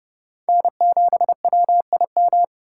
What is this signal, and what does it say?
Morse code, heard as a keyed tone, about 27 wpm, 700 Hz. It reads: N7WIM